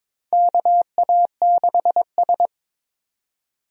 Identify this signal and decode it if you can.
Morse code, heard as a keyed tone, about 22 wpm, 700 Hz. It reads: KA6S
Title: USA Call Signs - 4 Characters - 22wpm